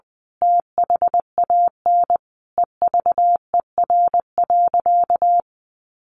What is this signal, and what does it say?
Morse code, heard as a keyed tone, about 20 wpm, 700 Hz. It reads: THAN EVER.
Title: Sentences from Top 1200 Words 20wpm